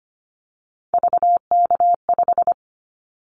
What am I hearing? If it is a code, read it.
Morse code, heard as a keyed tone, about 25 wpm, 700 Hz. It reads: VK5
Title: Callsign Prefixes for Top 50 DXCC - 25wpm